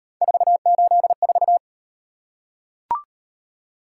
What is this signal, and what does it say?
Morse code, heard as a keyed tone, about 38 wpm, 700 Hz. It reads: 484